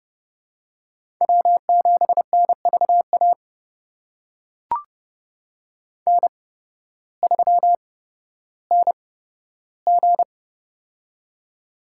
Morse code, heard as a keyed tone, about 30 wpm, spaced as for 14 wpm, 700 Hz. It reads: W7NVA N3NG
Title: USA Call Signs - 8x Character Spacing - 30wpm